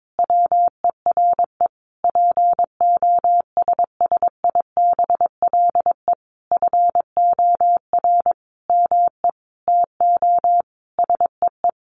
Morse code, heard as a keyed tone, about 22 wpm, 700 Hz. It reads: WERE POSSIBLE FOR ME TO SEE
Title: Sentences from Top 500 Words 22wpm